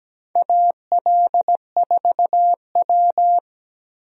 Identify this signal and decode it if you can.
Morse code, heard as a keyed tone, about 17 wpm, 700 Hz. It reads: AL4W
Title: USA Call Signs - 4 Characters - Repeated 2x - 17wpm